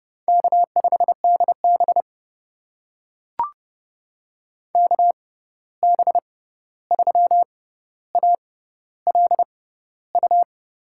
Morse code, heard as a keyed tone, about 32 wpm, spaced as for 14 wpm, 700 Hz. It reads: K5DB KB3ALU
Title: USA Call Signs - 6x Character Spacing - 30wpm